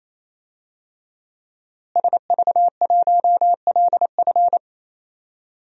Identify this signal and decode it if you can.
Morse code, heard as a keyed tone, about 28 wpm, 700 Hz. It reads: SV1LF
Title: DXCC-100 Call Signs - 5 Characters - Repeated 3x - 28wpm